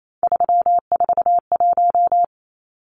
Morse code, heard as a keyed tone, about 28 wpm, 700 Hz. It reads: 341
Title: Numbers - 3-Digit 28wpm